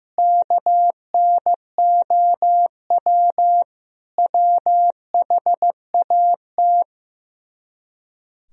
Morse code, heard as a keyed tone, about 15 wpm, 700 Hz. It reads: KNOW WHAT
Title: Speed-Racing - Sets of 5 Words 15wpm